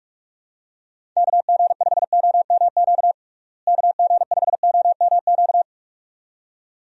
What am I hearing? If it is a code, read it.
Morse code, heard as a keyed tone, about 42 wpm, 700 Hz. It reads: KG5OMQ KG5OMQ
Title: USA Call Signs - 6 Characters - Repeated 2x - 45wpm